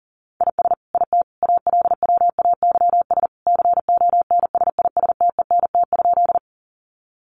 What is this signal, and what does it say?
Morse code, heard as a keyed tone, about 40 wpm, 700 Hz. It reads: IS IT ALWAYS CONSISTENT?
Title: Sentences from Top 1500 Words 40wpm